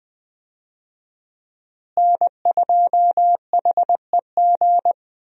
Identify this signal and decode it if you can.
Morse code, heard as a keyed tone, about 20 wpm, 700 Hz. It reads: N2HEG
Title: USA Call Signs - 5 Characters - Repeated 3x - 20wpm